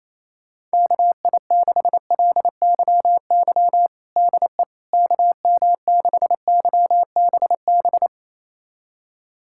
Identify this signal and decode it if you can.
Morse code, heard as a keyed tone, about 28 wpm, 700 Hz. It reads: KI6LYY DE KM6YBB
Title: USA Callsign DE Callsign - 6 Characters - 28wpm